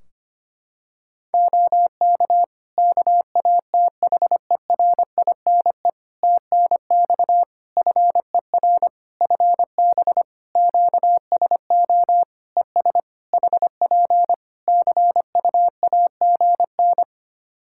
Morse code, heard as a keyed tone, about 25 wpm, 700 Hz. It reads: OK KATHERINE TNX FER FB QSO ES HP CUAGN
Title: K1USN SST QSO Practice 25wpm